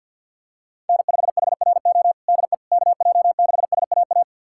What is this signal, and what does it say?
Morse code, heard as a keyed tone, about 50 wpm, 700 Hz. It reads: N5HRO DE KJ6SAA
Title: USA Callsign DE Callsign 50wpm